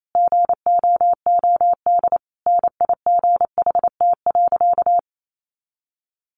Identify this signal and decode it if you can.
Morse code, heard as a keyed tone, about 28 wpm, 700 Hz. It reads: GOOD NIGHT.
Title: Sentences from Top 300 Words 28wpm